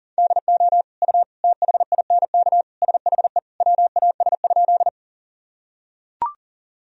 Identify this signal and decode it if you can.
Morse code, heard as a keyed tone, about 37 wpm, 700 Hz. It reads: DO U THINK SHE WAS?
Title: Sentences from Top 200 Words and Abbreviations 40wpm